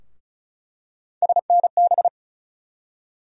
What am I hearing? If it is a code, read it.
Morse code, heard as a keyed tone, about 36 wpm, 700 Hz. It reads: SNB
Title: USA Call Sign Postfixes - 3 Letters 35wpm